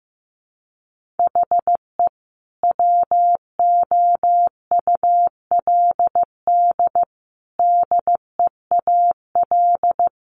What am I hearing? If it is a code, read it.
Morse code, heard as a keyed tone, about 15 wpm, 700 Hz. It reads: HE WOULD DEAL
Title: Sentences from Top 1400 Words 15wpm